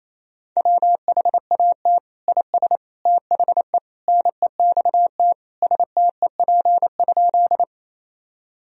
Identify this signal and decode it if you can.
Morse code, heard as a keyed tone, about 27 wpm, 690 Hz. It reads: WHAT IS THE NEXT STEP?
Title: Sentences from Top 1000 Words - Review Part 2 - 28wpm